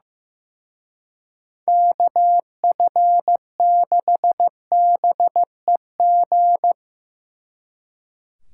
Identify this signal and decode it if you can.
Morse code, heard as a keyed tone, about 17 wpm, 700 Hz. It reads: KF6BEG
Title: Speed Racing - USA 2x3 Call Signs 15wpm